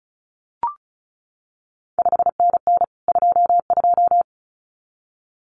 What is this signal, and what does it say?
Morse code, heard as a keyed tone, about 36 wpm, 700 Hz. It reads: 5NN 22
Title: CQ WW DX Contest Exchange Encore - 35wpm